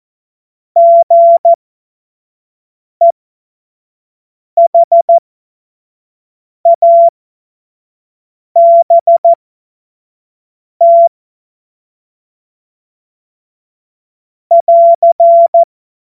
Morse code, heard as a keyed tone, about 14 wpm, spaced as for 5 wpm, 680 Hz. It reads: GEHABT <AR>
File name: Witz-19-August-2025-25-F70